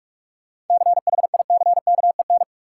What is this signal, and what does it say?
Morse code, heard as a keyed tone, about 44 wpm, 700 Hz. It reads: CHICKEN
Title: Sentences from Top 1000 Words - Review Part 5 - New Word Drill - 45wpm